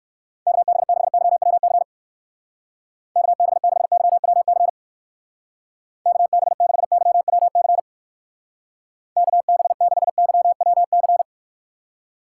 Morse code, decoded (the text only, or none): KB6YWC KB6YWC KB6YWC KB6YWC